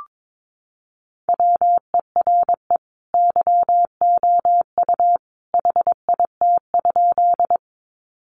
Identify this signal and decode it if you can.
Morse code, heard as a keyed tone, about 22 wpm, 700 Hz. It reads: WERE YOU HIT?
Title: Sentences from Top 1200 Words 22wpm